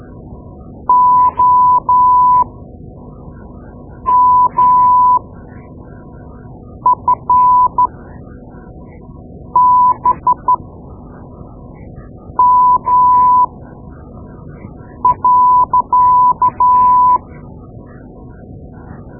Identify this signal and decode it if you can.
Morse code, heard as a keyed tone, about 13 wpm, spaced as for 4 wpm, 1000 Hz. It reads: OMFBM.
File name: Morse Lessons / Morse Lessons / Tape-2B